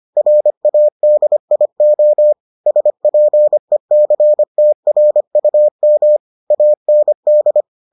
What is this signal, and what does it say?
Morse code, heard as a keyed tone, about 25 wpm, 600 Hz. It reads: RADIO SPECTRUM AND